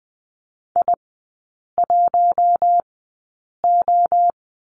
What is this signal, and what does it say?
Morse code, heard as a keyed tone, about 20 wpm, spaced as for 8 wpm, 700 Hz. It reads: I1O
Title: 152 Lesson 25e - Letter p - Three Characters 20wpm